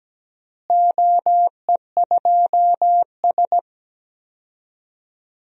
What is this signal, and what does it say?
Morse code, heard as a keyed tone, about 17 wpm, 700 Hz. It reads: OE2S
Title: DXCC-100 Call Signs - 4 Characters - Repeated 3x - 17wpm